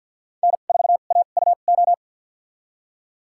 Morse code, heard as a keyed tone, about 47 wpm, spaced as for 35 wpm, 700 Hz. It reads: N4AUO
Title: USA Call Signs - 5 Characters - 2x Character Spacing - 50wpm